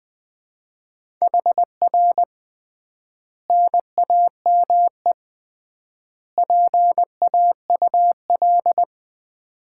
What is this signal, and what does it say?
Morse code, heard as a keyed tone, about 21 wpm, 700 Hz. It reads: HR NAME PAUL
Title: Common QSO Phrases - 3x Word-Spacing - 20wpm